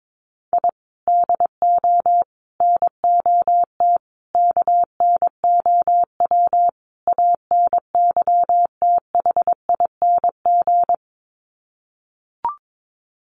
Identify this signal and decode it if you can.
Morse code, heard as a keyed tone, about 22 wpm, 700 Hz. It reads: I DO NOT KNOW ANYTHING
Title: Sets of 5 Words Encore 22wpm